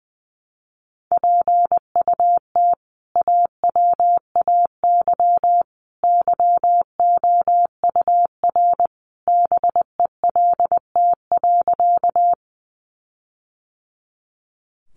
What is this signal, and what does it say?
Morse code, heard as a keyed tone, about 20 wpm, 700 Hz. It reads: PUT AWAY YOUR BELT.
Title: Sentences from Top 1400 Words 20wpm